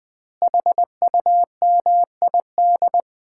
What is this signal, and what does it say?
Morse code, heard as a keyed tone, about 20 wpm, 700 Hz. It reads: HUMID
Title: Common QSO Phrases - 2x Word-Spacing - 20wpm